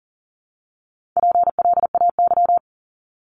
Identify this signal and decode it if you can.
Morse code, heard as a keyed tone, about 40 wpm, 690 Hz. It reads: PLAY
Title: Multi-letter Phonemes Repeated 3x 40wpm